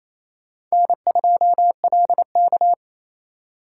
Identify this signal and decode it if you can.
Morse code, heard as a keyed tone, about 28 wpm, 700 Hz. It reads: N2LK
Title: K1USN SST QSO Practice - 28wpm